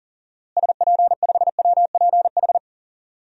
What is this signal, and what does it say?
Morse code, heard as a keyed tone, about 40 wpm, 700 Hz. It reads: SP5WPH
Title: DXCC-100 Call Signs - 6 Characters - Repeated 3x - 40wpm